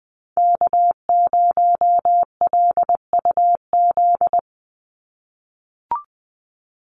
Morse code, heard as a keyed tone, about 20 wpm, 700 Hz. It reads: K0LUZ
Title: Call Signs - 20wpm